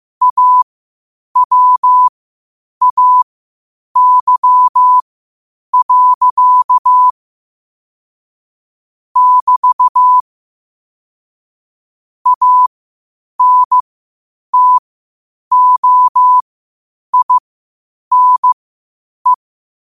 Morse code, heard as a keyed tone, about 15 wpm, spaced as for 8 wpm, 1000 Hz. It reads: AWAY. <BT> ANTOINE
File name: qotdfiles / 8wpm